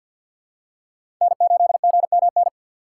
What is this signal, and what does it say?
Morse code, heard as a keyed tone, about 47 wpm, 700 Hz. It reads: N8GMN
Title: USA Call Signs - 5 Characters - 3x Character Spacing - 50wpm